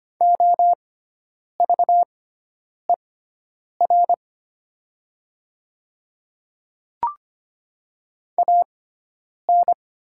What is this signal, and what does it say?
Morse code, heard as a keyed tone, about 25 wpm, spaced as for 9 wpm, 700 Hz. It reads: OVER AN
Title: 6x Character Spacing - Top 100 Words - 25wpm